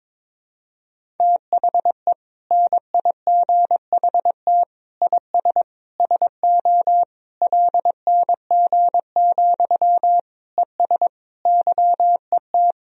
Morse code, heard as a keyed tone, about 22 wpm, 700 Hz. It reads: THE NIGHT IS SO LNG, ES YET
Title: Sentences from Top 500 Words and Abbreviations 22wpm